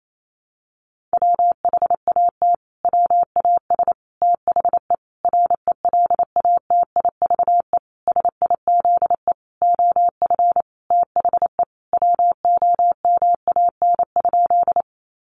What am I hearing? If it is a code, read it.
Morse code, heard as a keyed tone, about 28 wpm, 700 Hz. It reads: WHAT WAS THE RELATIVE SIZE OF THE WOMAN?
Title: Sentences from Top 1400 Words 28wpm